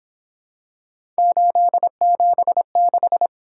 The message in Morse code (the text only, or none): 876